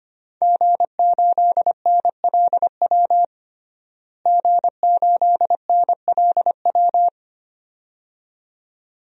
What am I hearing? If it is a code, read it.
Morse code, heard as a keyed tone, about 25 wpm, 700 Hz. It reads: G8NLW G8NLW
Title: DXCC-100 Call Signs - 5 Characters - Repeated 2x - 25wpm